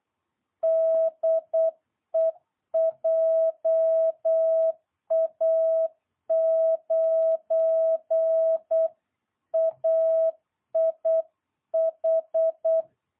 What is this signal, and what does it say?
Morse code, heard as a keyed tone, about 8 wpm, 650 Hz. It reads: DE JA9AIH